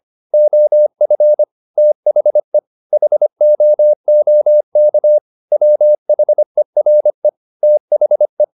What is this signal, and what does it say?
Morse code, heard as a keyed tone, about 25 wpm, 600 Hz. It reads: OF THE HOOK WHERE THE
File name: the old man and the sea part 1 25wpm